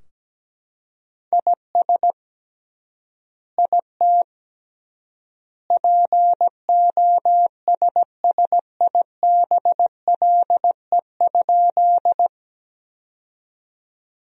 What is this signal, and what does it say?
Morse code, heard as a keyed tone, about 17 wpm, 700 Hz. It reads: IS IT POSSIBLE?
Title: 3x Word-Spacing - Sentences from Top 500 Words - 17wpm